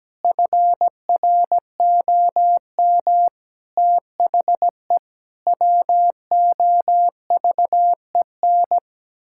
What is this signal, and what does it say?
Morse code, heard as a keyed tone, about 17 wpm, 700 Hz. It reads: FROM THE WOVEN
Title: Pangram Challenge v2 - 17wpm